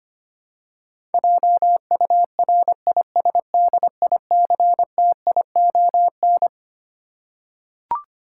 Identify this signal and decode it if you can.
Morse code, heard as a keyed tone, about 25 wpm, 700 Hz. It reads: JURISDICTION
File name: Twelve Letter Words - 25wpm